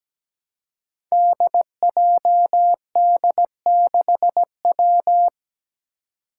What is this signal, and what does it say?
Morse code, heard as a keyed tone, about 17 wpm, 700 Hz. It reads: DJD6W
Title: Five Challenging Letter-Number Combinations 17wpm